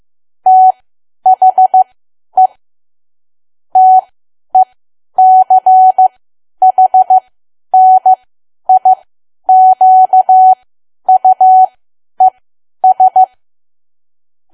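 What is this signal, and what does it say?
Morse code, heard as a keyed tone, about 15 wpm, spaced as for 10 wpm, 750 Hz. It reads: THE TECHNIQUES
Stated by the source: Morse learning practice at 10 wpm.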